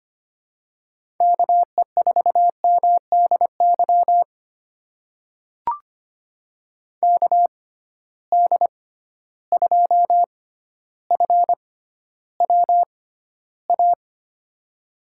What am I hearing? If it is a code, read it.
Morse code, heard as a keyed tone, about 25 wpm, spaced as for 13 wpm, 700 Hz. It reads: KE4MDY KD2FWA
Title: USA Call Signs - 6 Characters - 6x Character Spacing - 25wpm